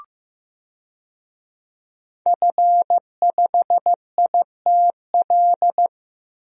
Morse code, heard as a keyed tone, about 15 wpm, 700 Hz. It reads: F5ITL
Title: DXCC-100 Call Signs - 5 Characters - 15wpm